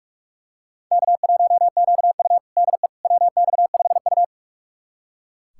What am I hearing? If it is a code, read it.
Morse code, heard as a keyed tone, about 42 wpm, 700 Hz. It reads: K1QU DE WX5U
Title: USA Callsign DE Callsign - 4 Characters - 45wpm